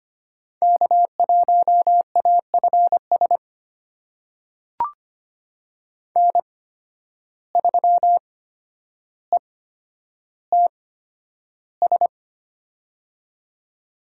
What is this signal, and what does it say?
Morse code, heard as a keyed tone, about 25 wpm, spaced as for 11 wpm, 700 Hz. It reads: K1AFS N3ETS
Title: USA Call Signs - 5 Characters - 8x Character Spacing - 25wpm